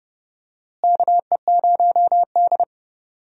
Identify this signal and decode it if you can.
Morse code, heard as a keyed tone, about 32 wpm, 700 Hz. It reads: KE0D
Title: USA Call Signs - 4 Characters - Repeated 3x - 30wpm